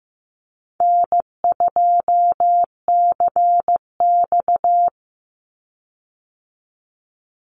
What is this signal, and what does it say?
Morse code, heard as a keyed tone, about 15 wpm, 700 Hz. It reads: N2CX
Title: Easy Call Signs Repeated 2x - 15wpm